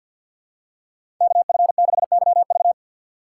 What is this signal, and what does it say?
Morse code, heard as a keyed tone, about 47 wpm, 690 Hz. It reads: KF6YV
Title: USA Call Signs - 5 Characters - 3x Character Spacing - 50wpm